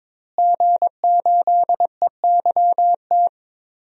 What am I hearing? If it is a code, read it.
Morse code, heard as a keyed tone, about 22 wpm, 700 Hz. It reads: G8EYT
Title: DXCC-100 Call Signs - 5 Characters - Repeated 3x - 22wpm